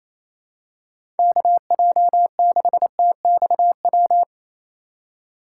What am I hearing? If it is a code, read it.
Morse code, heard as a keyed tone, about 28 wpm, 700 Hz. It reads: KJ6TXW
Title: USA Call Signs - 6 Characters - 3x Character Spacing - 28wpm